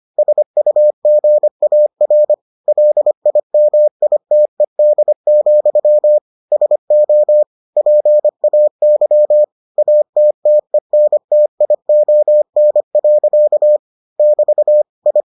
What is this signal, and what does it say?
Morse code, heard as a keyed tone, about 25 wpm, 600 Hz. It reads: SUGAR LIMITED, SO PAY ATTENTION. <BT> I